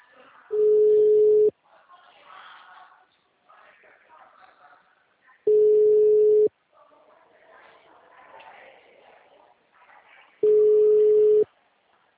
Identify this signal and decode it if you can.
Ringback tone (Europe, 425 Hz), with speech.